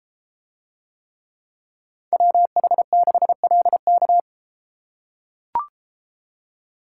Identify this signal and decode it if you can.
Morse code, heard as a keyed tone, about 32 wpm, 700 Hz. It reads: WH6LK